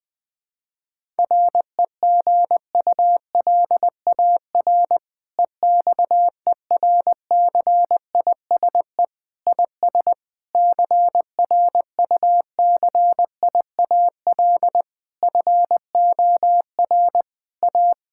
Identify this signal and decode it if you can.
Morse code, heard as a keyed tone, about 21 wpm, 700 Hz. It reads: REGULAR EXERCISE IS CRUCIAL FOR A
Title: Difficult Sentences with Hints - 20wpm